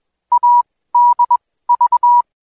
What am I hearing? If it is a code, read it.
Morse code, heard as a keyed tone, about 21 wpm, 970 Hz. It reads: A D V